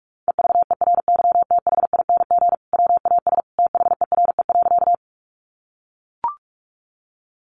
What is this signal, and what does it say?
Morse code, heard as a keyed tone, about 44 wpm, 700 Hz. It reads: EVERYTHING WAS THERE.
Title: Sentences from Top 700 Words 45wpm